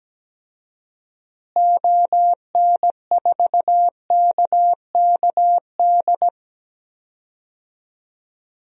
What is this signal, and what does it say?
Morse code, heard as a keyed tone, about 17 wpm, 700 Hz. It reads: ON4KKD
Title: DXCC-100 Call Signs - 6 Characters - Repeated 2x - 17wpm